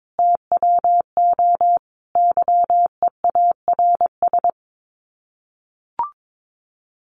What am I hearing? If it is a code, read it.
Morse code, heard as a keyed tone, about 22 wpm, 700 Hz. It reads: TWO YEARS
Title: Sets of 5 Words 22wpm